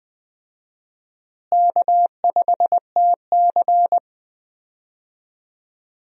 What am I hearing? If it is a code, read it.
Morse code, heard as a keyed tone, about 20 wpm, 700 Hz. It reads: K5TC
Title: USA Call Signs - 4 Characters - Repeated 2x - 20wpm